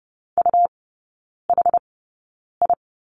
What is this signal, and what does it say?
Morse code, heard as a keyed tone, about 30 wpm, spaced as for 9 wpm, 700 Hz. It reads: UHI
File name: Lesson 18e - Letter m - Three Characters 30wpm